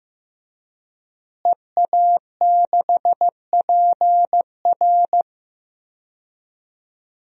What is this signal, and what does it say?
Morse code, heard as a keyed tone, about 16 wpm, 700 Hz. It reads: EA6PR
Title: DXCC-100 Call Signs - 5 Characters - Repeated 3x - 15wpm